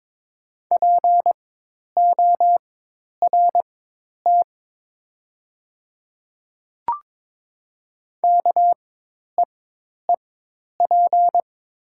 Morse code, heard as a keyed tone, about 22 wpm, spaced as for 10 wpm, 700 Hz. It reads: PORT KEEP
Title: 4x Character Spacing - Top 300 Words - 22wpm